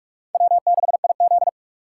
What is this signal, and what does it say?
Morse code, heard as a keyed tone, about 43 wpm, 700 Hz. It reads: W6IZ
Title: USA Call Signs - 4 Characters - 3x Character Spacing - 45wpm